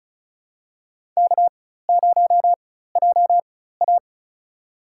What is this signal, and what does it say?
Morse code, heard as a keyed tone, about 34 wpm, spaced as for 16 wpm, 700 Hz. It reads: K0JA
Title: USA Call Signs - 4x Character Spacing - 35wpm